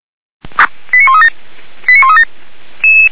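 Five-tone selective call (ZVEI), heard as quick runs of five tones, tones of about 80 ms. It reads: 89137 89137
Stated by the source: ZVEI five-tone selective call received on SDR.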